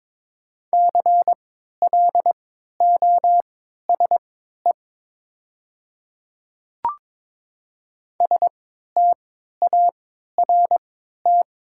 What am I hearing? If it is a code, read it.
Morse code, heard as a keyed tone, about 22 wpm, spaced as for 13 wpm, 700 Hz. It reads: CLOSE START
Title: 3x Character Spacing - Top 300 Words - 22wpm